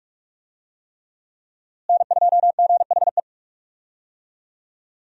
Morse code, heard as a keyed tone, about 41 wpm, 700 Hz. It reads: N2GHE
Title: USA Call Signs - 5 Characters - Repeated 3x - 45wpm